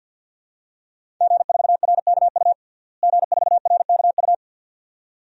Morse code, heard as a keyed tone, about 52 wpm, 700 Hz. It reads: G4RKU G4RKU
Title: DXCC-100 Call Signs - 5 Characters - Repeated 2x - 50wpm